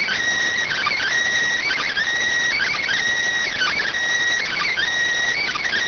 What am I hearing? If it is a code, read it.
Emergency Alert System (SAME) tones (header bursts), heard as harsh buzzing bursts. It is heard from 0.6 to 4.8 s.